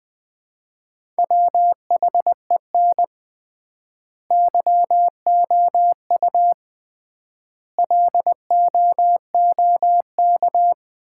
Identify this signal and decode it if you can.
Morse code, heard as a keyed tone, about 21 wpm, 700 Hz. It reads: WHEN YOU LOOK
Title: 3x Word-Spacing - Sets of 5 Words - 20wpm